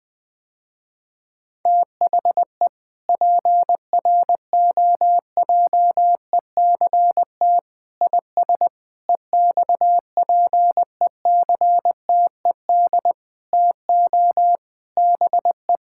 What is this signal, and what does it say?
Morse code, heard as a keyed tone, about 20 wpm, 700 Hz. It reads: THE PROJECT IS EXPECTED TO BE
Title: Sentences with 8-Word Phrases - Spoken Beginning - 20wpm